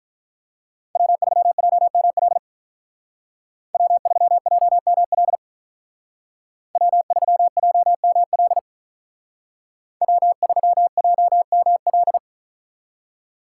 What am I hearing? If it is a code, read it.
Morse code, heard as a keyed tone, about 43 wpm, 700 Hz. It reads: W3JML W3JML W3JML W3JML
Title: Speed Racing - USA 1x3 Call Signs 35wpm